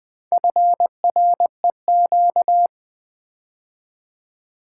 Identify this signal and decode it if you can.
Morse code, heard as a keyed tone, about 20 wpm, 700 Hz. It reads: FREQ
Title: Build A Word - 2 Letters At A Time - 8-Letter Words - 20wpm